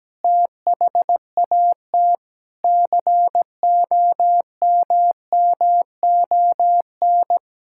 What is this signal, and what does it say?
Morse code, heard as a keyed tone, about 17 wpm, 700 Hz. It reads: THAT COMMON
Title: Sentences from Top 1000 Words - Review Part 5 - 17wpm